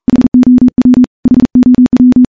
Morse code, heard as a keyed tone, about 42 wpm, 250 Hz. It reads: HOW HOW
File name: morse / morse-training-40WPM